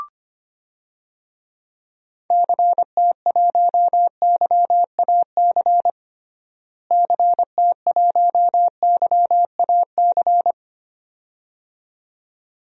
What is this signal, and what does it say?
Morse code, heard as a keyed tone, about 25 wpm, 700 Hz. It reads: CT1YAC CT1YAC
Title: DXCC-100 Call Signs - 6 Characters - Repeated 2x - 25wpm